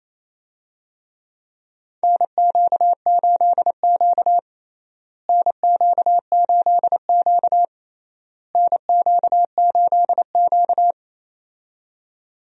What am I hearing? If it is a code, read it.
Morse code, heard as a keyed tone, about 28 wpm, 700 Hz. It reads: NQ8Q NQ8Q NQ8Q
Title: USA Call Signs - 4 Characters - Repeated 3x - 28wpm